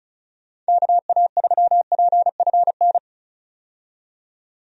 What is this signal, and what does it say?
Morse code, heard as a keyed tone, about 34 wpm, 700 Hz. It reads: KA3PFN
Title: USA Call Signs - 6 Characters - 35wpm